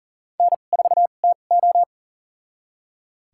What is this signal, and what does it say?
Morse code, heard as a keyed tone, about 40 wpm, spaced as for 29 wpm, 700 Hz. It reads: N4TO
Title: USA Call Signs - 4 Characters - 2x Character Spacing - 40wpm